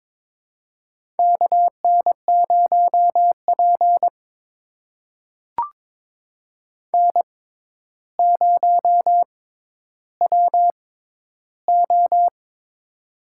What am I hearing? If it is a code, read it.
Morse code, heard as a keyed tone, about 22 wpm, spaced as for 11 wpm, 700 Hz. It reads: KN0P N0WO
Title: USA Call Signs - 4 Characters - 6x Character Spacing - 22wpm